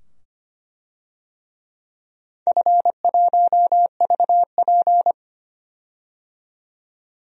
Morse code, heard as a keyed tone, about 25 wpm, 700 Hz. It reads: F1VP
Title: DXCC-100 Call Signs - 4 Characters - 25wpm